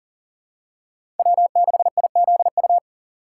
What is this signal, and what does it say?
Morse code, heard as a keyed tone, about 40 wpm, 700 Hz. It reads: W6IZU
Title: USA Call Signs - 5 Characters - 2x Character Spacing - 40wpm